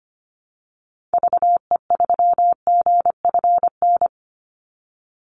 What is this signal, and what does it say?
Morse code, heard as a keyed tone, about 25 wpm, 700 Hz. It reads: VE3GFN
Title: Call Signs - 25wpm